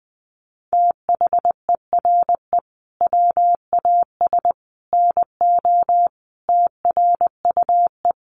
Morse code, heard as a keyed tone, about 20 wpm, 700 Hz. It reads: THERE WAS NO TRUE
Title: Sentences from Top 1300 Words 20wpm